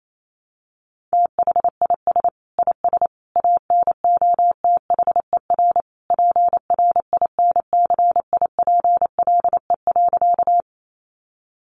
Morse code, heard as a keyed tone, about 28 wpm, 700 Hz. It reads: THIS IS ANOTHER PRINCIPLE.
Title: Sentences from Top 1400 Words 28wpm